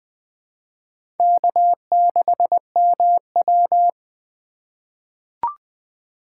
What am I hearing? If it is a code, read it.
Morse code, heard as a keyed tone, about 21 wpm, 700 Hz. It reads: K6MW
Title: 2023 SOTA Chaser Call Signs - 20wpm